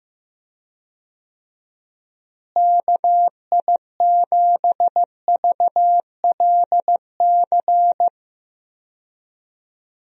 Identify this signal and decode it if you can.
Morse code, heard as a keyed tone, about 15 wpm, 700 Hz. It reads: KI7VLC
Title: USA Call Signs - 6 Characters - Repeated 3x - 15wpm